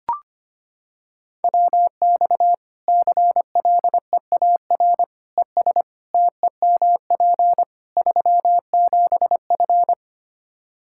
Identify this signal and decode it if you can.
Morse code, heard as a keyed tone, about 25 wpm, 700 Hz. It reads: WX CLEAR ES TEMP 37F
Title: K1USN SST QSO Practice 25wpm